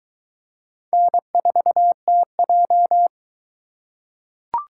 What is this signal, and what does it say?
Morse code, heard as a keyed tone, about 21 wpm, 700 Hz. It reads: N4TJ